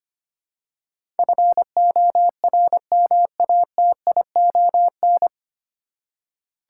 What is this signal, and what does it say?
Morse code, heard as a keyed tone, about 25 wpm, 690 Hz. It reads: FORMATION
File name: practice / 25wpm / 9 Letter Words - First 3 Letters Given - 25wpm